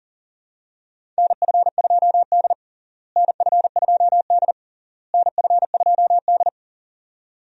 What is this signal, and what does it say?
Morse code, heard as a keyed tone, about 40 wpm, 700 Hz. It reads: NF2D NF2D NF2D
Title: USA Call Signs - 4 Characters - Repeated 3x - 40wpm